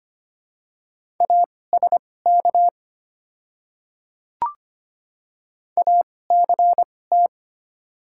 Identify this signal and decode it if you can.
Morse code, heard as a keyed tone, about 25 wpm, spaced as for 18 wpm, 690 Hz. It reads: ASK ACT
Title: 2x Character Spacing - Top 200 Words - 25wpm